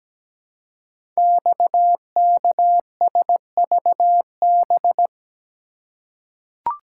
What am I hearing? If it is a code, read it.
Morse code, heard as a keyed tone, about 17 wpm, 700 Hz. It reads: XKSVB
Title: Code Groups - 5 Letters - 17wpm